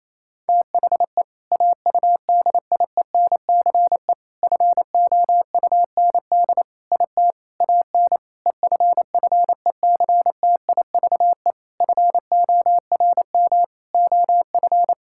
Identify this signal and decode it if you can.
Morse code, heard as a keyed tone, about 28 wpm, 700 Hz. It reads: THE AUDIENCE FOUND IT AN EFFECTIVE FORM OF
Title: Sentences from Top 1000 Words - Review Part 6 - 28wpm